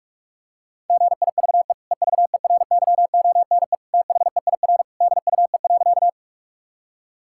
Morse code, heard as a keyed tone, about 41 wpm, 700 Hz. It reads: GIVE EVERYONE THEIR DUE.
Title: Sentences from Top 1000 Words - Review Part 6 - 45wpm